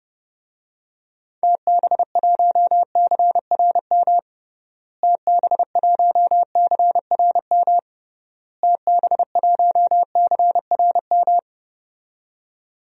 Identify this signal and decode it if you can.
Morse code, heard as a keyed tone, about 30 wpm, 700 Hz. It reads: TB1CRM TB1CRM TB1CRM
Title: DXCC-100 Call Signs - 6 Characters - Repeated 3x - 30wpm